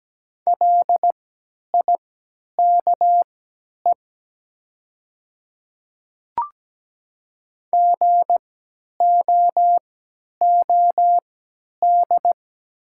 Morse code, heard as a keyed tone, about 17 wpm, spaced as for 10 wpm, 700 Hz. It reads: LIKE GOOD
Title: 3x Character Spacing - Top 500 Words - 17wpm